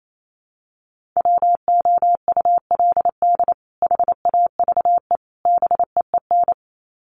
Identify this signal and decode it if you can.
Morse code, heard as a keyed tone, about 28 wpm, 700 Hz. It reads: WOULD HAVE BEEN
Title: Sets of 3 Words 28wpm